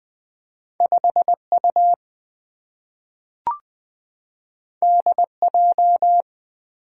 Morse code, heard as a keyed tone, about 20 wpm, 690 Hz. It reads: 5U DJ
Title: Two Letter-Number Combinations - 20wpm